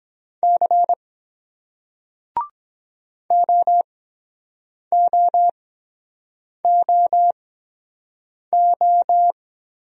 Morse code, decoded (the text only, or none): C OOOO